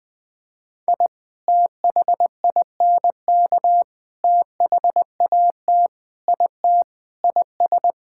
Morse code, heard as a keyed tone, about 21 wpm, 700 Hz. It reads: I THINK THAT IT IS
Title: Sentences with 3-Word Phrases - 20wpm